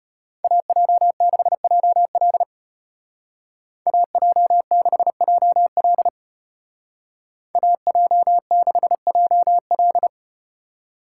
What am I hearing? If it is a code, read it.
Morse code, heard as a keyed tone, about 33 wpm, 700 Hz. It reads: AJ6JL AJ6JL AJ6JL